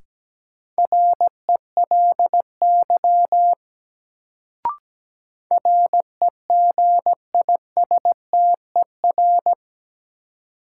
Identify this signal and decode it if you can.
Morse code, heard as a keyed tone, about 17 wpm, 700 Hz. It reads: RELY REGISTER
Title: Sentences from Top 1200 Words - New Word Drill Alt - 17wpm